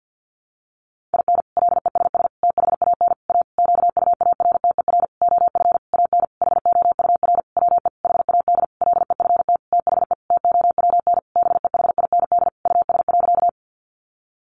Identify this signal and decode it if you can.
Morse code, heard as a keyed tone, about 50 wpm, 700 Hz. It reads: IN LESS THAN A QUARTER OF AN HOUR WE HAD LEFT THE TOWN BEHIND US.
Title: Sentences from Top 1300 Words 50wpm